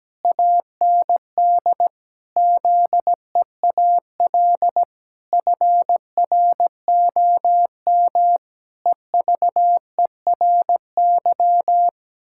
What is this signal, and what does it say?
Morse code, heard as a keyed tone, about 17 wpm, 700 Hz. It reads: AND ZEAL FROM EVERY
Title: Pangram Challenge v2 - 17wpm